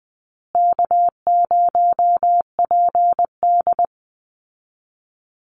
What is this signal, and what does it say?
Morse code, heard as a keyed tone, about 20 wpm, 700 Hz. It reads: K0PD
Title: Easy Call Signs Repeated 2x - 20wpm